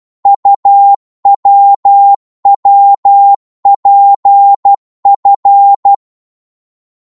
Morse code, heard as a keyed tone, about 12 wpm, 800 Hz. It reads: UWWPF
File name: Morse / ltr-12wpm